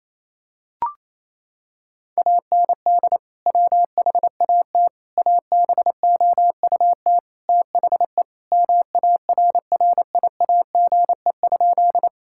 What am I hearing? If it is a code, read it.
Morse code, heard as a keyed tone, about 28 wpm, 700 Hz. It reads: AND WHAT ABOUT THE MARRIAGE?
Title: Sentences from Top 1000 Words - Review Part 5 - 28wpm